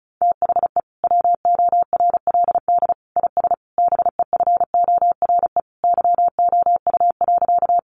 Morse code, heard as a keyed tone, about 34 wpm, 700 Hz. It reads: THE WORLD IS BEFORE YOU.
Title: Sentences from Top 200 Words 35wpm